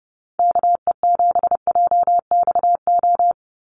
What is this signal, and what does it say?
Morse code, heard as a keyed tone, about 30 wpm, 700 Hz. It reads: KE7JXO
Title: USA Call Signs 30wpm